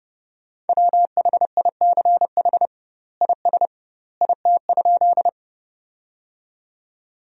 Morse code, heard as a keyed tone, about 30 wpm, 700 Hz. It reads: WHICH IS IT?
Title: 2x Word-Spacing - Sentences from Top 100 Words - 30wpm